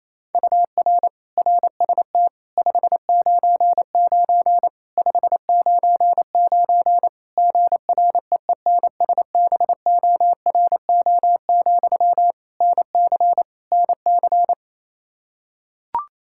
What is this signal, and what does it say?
Morse code, heard as a keyed tone, about 28 wpm, 700 Hz. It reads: UR RST 599 599 GREENSBORO, NC NC
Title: K1USN SST QSO Practice 28wpm